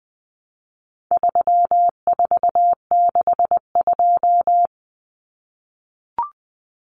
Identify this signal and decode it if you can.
Morse code, heard as a keyed tone, about 20 wpm, 700 Hz. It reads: 3462
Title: Common Cut Numbers - 4-Digit 20wpm